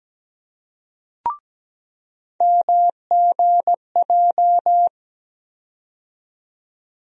Morse code, heard as a keyed tone, about 17 wpm, 700 Hz. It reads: MGJ
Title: USA Call Sign Postfixes - 3 Letters 17wpm